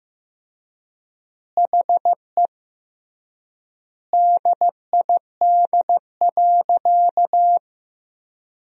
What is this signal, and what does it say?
Morse code, heard as a keyed tone, about 15 wpm, 700 Hz. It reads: HE DID.
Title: 3x Word-Spacing - Sentences from Top 500 Words - 15wpm